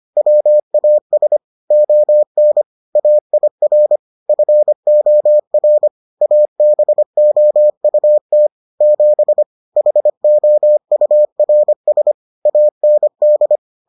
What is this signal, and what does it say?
Morse code, heard as a keyed tone, about 25 wpm, 600 Hz. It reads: WAS ON AIR FOR ABOUT 7 HOURS AND